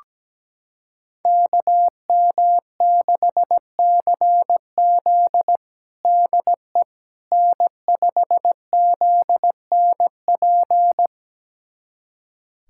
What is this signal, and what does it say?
Morse code, heard as a keyed tone, about 17 wpm, 700 Hz. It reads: KM6CZ DE N5ZNP
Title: USA Callsign DE Callsign - 5 Characters - 17wpm